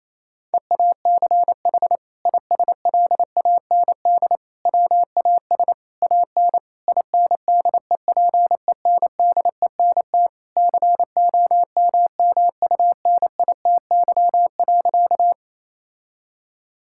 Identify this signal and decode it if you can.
Morse code, heard as a keyed tone, about 28 wpm, 700 Hz. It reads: EACH ISLAND WAS AN INDEPENDENT COMMUNITY.
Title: Sentences from Top 1000 Words - Review Part 6 - 28wpm